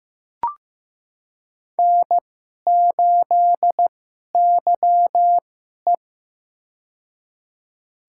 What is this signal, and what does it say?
Morse code, heard as a keyed tone, about 15 wpm, spaced as for 11 wpm, 700 Hz. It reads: N8YE
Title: USA Call Signs - 4 Characters - 2x Character Spacing - 15wpm